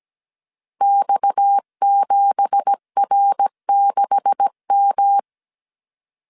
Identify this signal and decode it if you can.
Morse code, heard as a keyed tone, about 17 wpm, 800 Hz. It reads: X7R6M
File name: CW / DET / 1 9 1500C S A 16wpm 2828